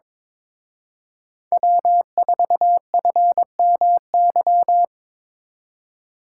Morse code, heard as a keyed tone, about 25 wpm, 700 Hz. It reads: W4FMY
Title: Speed Racing - USA 1x3 Call Signs 22wpm